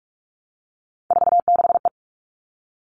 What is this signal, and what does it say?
Morse code, heard as a keyed tone, about 44 wpm, 700 Hz. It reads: 46E
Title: All Cut Numbers - 3-Digit 45wpm